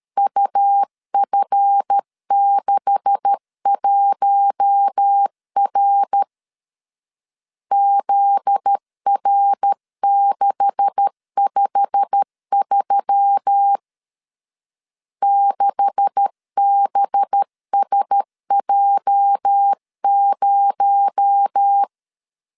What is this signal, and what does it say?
Morse code, heard as a keyed tone, about 13 wpm, 800 Hz. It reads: UF61R ZR653 6BSJ0
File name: LEZIONI / 9 286 12wpm 2305